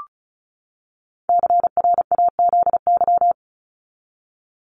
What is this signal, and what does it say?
Morse code, heard as a keyed tone, about 34 wpm, 700 Hz. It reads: CRAZY
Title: Top 1501-1750 Words 35wpm